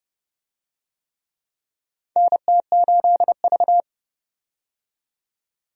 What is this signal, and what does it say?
Morse code, heard as a keyed tone, about 30 wpm, 700 Hz. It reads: NT8V
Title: USA Call Signs - 4 Characters - 30wpm